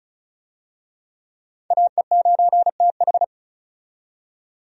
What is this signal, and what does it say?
Morse code, heard as a keyed tone, about 36 wpm, 700 Hz. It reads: AE9TH
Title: USA Call Signs - 5 Characters - 35wpm